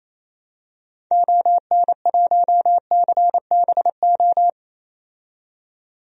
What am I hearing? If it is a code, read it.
Morse code, heard as a keyed tone, about 28 wpm, 700 Hz. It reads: ON1CBO
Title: DXCC-100 Call Signs - 6 Characters - 28wpm